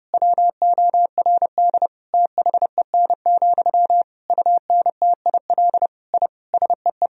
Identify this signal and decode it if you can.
Morse code, heard as a keyed tone, about 30 wpm, 700 Hz. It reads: WORD THEN, UNTIL I SEE
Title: Sentences from Top 1000 Words - Review Part 1 - 30wpm